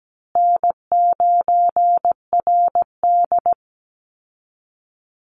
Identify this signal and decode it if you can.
Morse code, heard as a keyed tone, about 17 wpm, 700 Hz. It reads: N9RD
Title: Easy Call Signs Repeated 3x - 17wpm